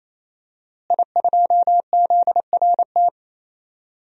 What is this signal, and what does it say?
Morse code, heard as a keyed tone, about 28 wpm, 700 Hz. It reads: I2ZRT
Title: DXCC-100 Call Signs - 5 Characters - Repeated 3x - 28wpm